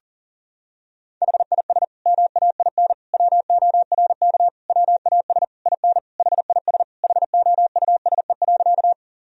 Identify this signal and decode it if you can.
Morse code, heard as a keyed tone, about 40 wpm, 700 Hz. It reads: HIS MAIN WORK WAS IN HIS HOUSE.
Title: Sentences from Top 1000 Words - Review Part 2 - 40wpm